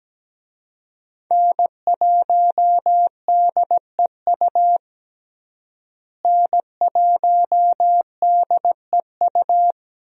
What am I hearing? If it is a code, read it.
Morse code, heard as a keyed tone, about 17 wpm, 700 Hz. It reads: N1DEU N1DEU
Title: USA Call Signs - 5 Characters - Repeated 2x - 17wpm